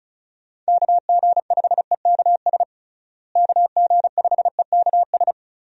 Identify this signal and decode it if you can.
Morse code, heard as a keyed tone, about 34 wpm, 700 Hz. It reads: KG5EKS KG5EKS
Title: USA Call Signs - 6 Characters - Repeated 2x - 35wpm